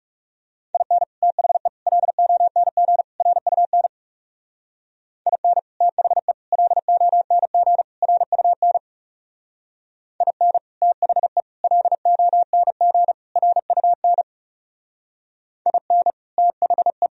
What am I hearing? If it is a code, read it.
Morse code, heard as a keyed tone, about 38 wpm, 700 Hz. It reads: IN THE LONG RUN IN THE LONG RUN IN THE LONG RUN IN THE